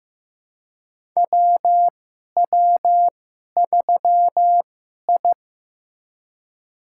Morse code, heard as a keyed tone, about 15 wpm, spaced as for 11 wpm, 700 Hz. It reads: WW3I
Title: USA Call Signs - 2x Character Spacing - 15wpm